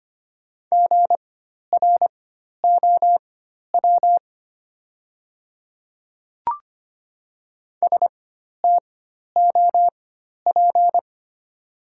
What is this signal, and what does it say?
Morse code, heard as a keyed tone, about 25 wpm, spaced as for 12 wpm, 690 Hz. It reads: GROW STOP
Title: 4x Character Spacing - Top 300 Words - 25wpm